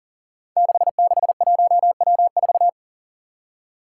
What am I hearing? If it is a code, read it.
Morse code, heard as a keyed tone, about 40 wpm, 700 Hz. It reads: 661W4
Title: Five Challenging Letter-Number Combinations 40wpm